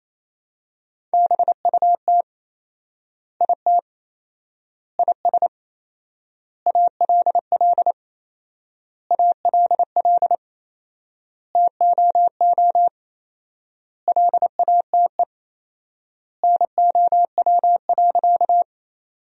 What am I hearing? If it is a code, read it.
Morse code, heard as a keyed tone, about 28 wpm, 700 Hz. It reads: BUT IT IS ALL ALL TOO LATE NOW.
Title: 4x Word-Spacing - Sentences from Top 300 Words - 28wpm